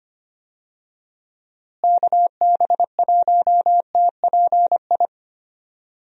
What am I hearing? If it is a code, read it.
Morse code, heard as a keyed tone, about 25 wpm, 700 Hz. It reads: KB1TPI
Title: USA Call Signs - 6 Characters - Repeated 3x - 25wpm